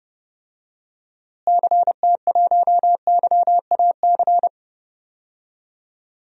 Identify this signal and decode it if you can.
Morse code, heard as a keyed tone, about 31 wpm, 700 Hz. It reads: CT1YAC
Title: DXCC-100 Call Signs - 6 Characters - Repeated 2x - 30wpm